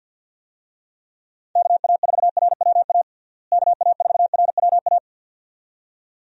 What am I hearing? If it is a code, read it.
Morse code, heard as a keyed tone, about 45 wpm, 700 Hz. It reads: KA4RWA KA4RWA
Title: USA Call Signs - 6 Characters - Repeated 2x - 50wpm